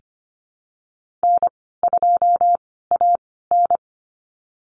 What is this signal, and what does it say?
Morse code, heard as a keyed tone, about 25 wpm, spaced as for 16 wpm, 700 Hz. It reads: N2AN
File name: Lesson 15g - Number 5 - Call Signs 25wpm